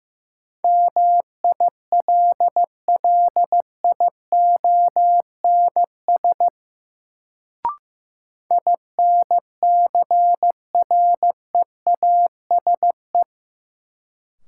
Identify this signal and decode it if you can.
Morse code, heard as a keyed tone, about 15 wpm, 700 Hz. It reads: MILLIONS INCREASE
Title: Sets of 5 Words - New Word Drill - 15wpm